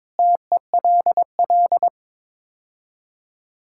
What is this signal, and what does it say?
Morse code, heard as a keyed tone, about 22 wpm, 700 Hz. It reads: TELL
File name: practice / 22wpm / Build A Phrase - 5-Word Phrases - 22wpm